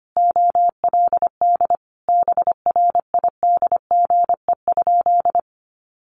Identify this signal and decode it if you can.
Morse code, heard as a keyed tone, about 25 wpm, 700 Hz. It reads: OLD BRIDGE?
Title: Sentences from Top 1200 Words 25wpm